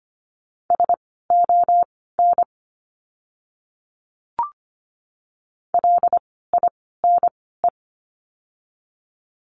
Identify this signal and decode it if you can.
Morse code, heard as a keyed tone, about 25 wpm, spaced as for 16 wpm, 700 Hz. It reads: SON LINE
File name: course / 25:10wpm / 094 Lesson 16f - Letter c - Words 25wpm